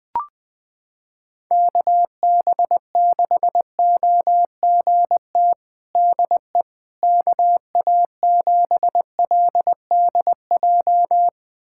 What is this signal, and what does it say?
Morse code, heard as a keyed tone, about 20 wpm, 700 Hz. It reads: KB6OGT DE KA7LDJ
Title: USA Callsign DE Callsign - 6 Characters - 20wpm